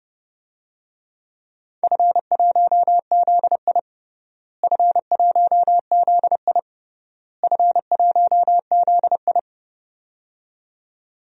Morse code, heard as a keyed tone, about 30 wpm, 700 Hz. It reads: F1ZI F1ZI F1ZI
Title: DXCC-100 Call Signs - 4 Characters - Repeated 3x - 30wpm